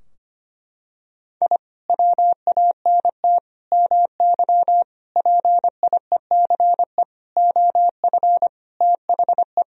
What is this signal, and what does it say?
Morse code, heard as a keyed tone, about 25 wpm, 700 Hz. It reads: I WANT MY PIECE OF THE
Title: Sentences from Top 1000 Words - Review Part 2 - 25wpm